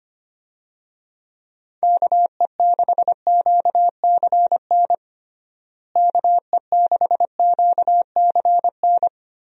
Morse code, heard as a keyed tone, about 25 wpm, 700 Hz. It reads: KE6QCN KE6QCN
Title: USA Call Signs - 6 Characters - Repeated 2x - 25wpm